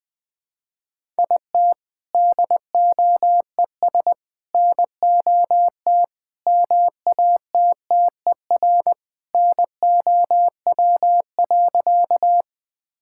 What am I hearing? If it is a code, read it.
Morse code, heard as a keyed tone, about 20 wpm, 700 Hz. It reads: IT DOES NOT MATTER NOW.
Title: Sentences with 4-Word Phrases - 20wpm